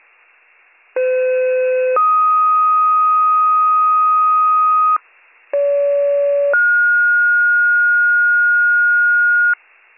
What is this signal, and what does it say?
Two-tone page, heard as a one-second tone, then a three-second tone, about 510 then 1250 Hz, about 570 then 1430 Hz, with noise. It is heard from 1.0 to 9.5 s.